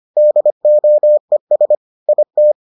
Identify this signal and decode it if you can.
Morse code, heard as a keyed tone, about 25 wpm, 600 Hz. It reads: DOES IT